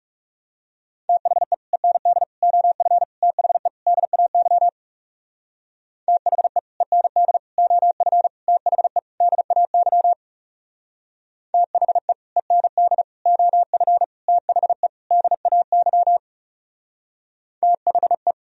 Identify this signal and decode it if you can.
Morse code, heard as a keyed tone, about 38 wpm, 700 Hz. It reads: THE END OF THE DAY THE END OF THE DAY THE END OF THE DAY THE